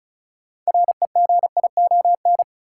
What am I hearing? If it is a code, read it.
Morse code, heard as a keyed tone, about 36 wpm, 700 Hz. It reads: REGION
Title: Sentences from Top 1500 Words - New Word Drill Alt - 35wpm